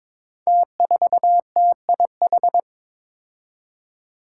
Morse code, heard as a keyed tone, about 22 wpm, 700 Hz. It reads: T4TIH
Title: Five Challenging Letter-Number Combinations 22wpm